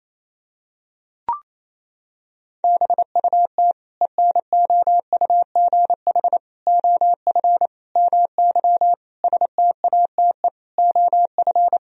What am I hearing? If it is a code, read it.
Morse code, heard as a keyed tone, about 28 wpm, 700 Hz. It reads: BUT ENOUGH OF MY STATE OF
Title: Sentences from Top 1000 Words - Review Part 2 - 28wpm